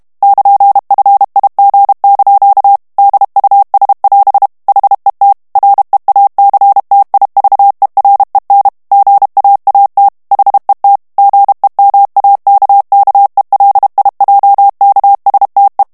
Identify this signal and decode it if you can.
Morse code, heard as a keyed tone, about 32 wpm, 800 Hz. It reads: <KN>FIG) DUS<AS> HET REACTIVEREN GAAT HET GEMAKKELIJKSTE
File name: HOLLANDAIS texte clair 2 vitesse 30 mots minute